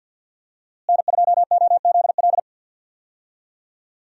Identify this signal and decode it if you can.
Morse code, heard as a keyed tone, about 51 wpm, 700 Hz. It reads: N2OZL
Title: USA Call Signs - 5 Characters - Repeated 3x - 50wpm